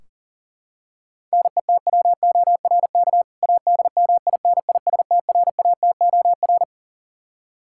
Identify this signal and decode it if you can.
Morse code, heard as a keyed tone, about 40 wpm, 700 Hz. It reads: NETWORK ADMINISTRATOR
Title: K1USN SST QSO Practice 40wpm